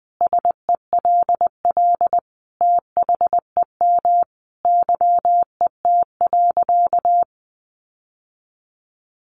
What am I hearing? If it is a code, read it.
Morse code, heard as a keyed tone, about 20 wpm, 700 Hz. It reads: SELL THEM YET.
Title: Sentences from Top 1100 Words 20wpm